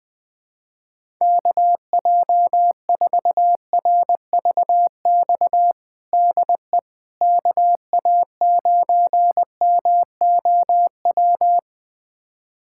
Morse code, heard as a keyed tone, about 20 wpm, 700 Hz. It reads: KJ4RVX DE KA9MOW
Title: USA Callsign DE Callsign - 6 Characters - 20wpm